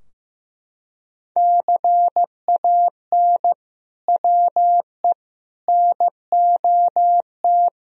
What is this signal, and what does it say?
Morse code, heard as a keyed tone, about 15 wpm, 700 Hz. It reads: CAN WE NOT
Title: Sentences from Top 1000 Words - Review Part 4 - 15wpm